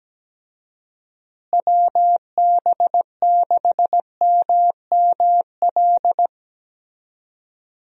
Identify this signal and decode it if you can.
Morse code, heard as a keyed tone, about 17 wpm, 700 Hz. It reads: WB6MML
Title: USA Call Signs - 6 Characters - Repeated 3x - 17wpm